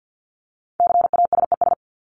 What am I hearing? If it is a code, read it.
Morse code, heard as a keyed tone, about 48 wpm, 700 Hz. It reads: CASES
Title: ES:S Suffix - Top 100 Words with Base Words 50wpm